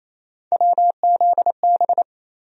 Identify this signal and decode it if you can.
Morse code, heard as a keyed tone, about 28 wpm, 700 Hz. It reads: WZB
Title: USA Call Sign Postfixes - 3 Letters 28wpm